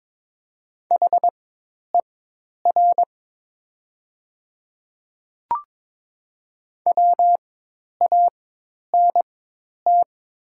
Morse code, heard as a keyed tone, about 22 wpm, spaced as for 10 wpm, 700 Hz. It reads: HER WANT
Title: 4x Character Spacing - Top 200 Words - 22wpm